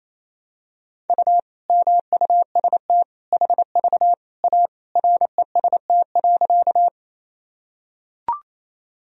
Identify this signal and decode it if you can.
Morse code, heard as a keyed tone, about 28 wpm, 700 Hz. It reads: U MUST HV A REST.
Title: Sentences from Top 500 Words and Abbreviations 28wpm